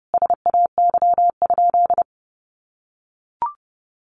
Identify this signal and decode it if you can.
Morse code, heard as a keyed tone, about 28 wpm, 700 Hz. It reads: SAY?
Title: Sentences from Top 800 Words 30wpm